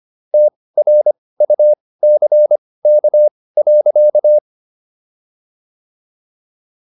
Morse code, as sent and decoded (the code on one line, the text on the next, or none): - .-. ..- -.-. -.- .-.-.-
TRUCK.